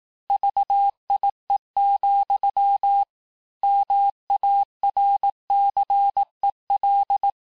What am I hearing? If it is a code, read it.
Morse code, heard as a keyed tone, about 18 wpm, 800 Hz. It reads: VIE, MARCEL
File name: cours-du-lundi files / cours-1594--teste-clair-vitesse-17mpm-puis-22mpm